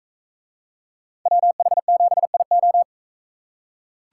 Morse code, heard as a keyed tone, about 40 wpm, 700 Hz. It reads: WH7IO